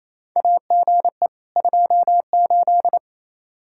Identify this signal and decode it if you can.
Morse code, heard as a keyed tone, about 28 wpm, 700 Hz. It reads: AGE 28
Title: K1USN SST QSO Practice 28wpm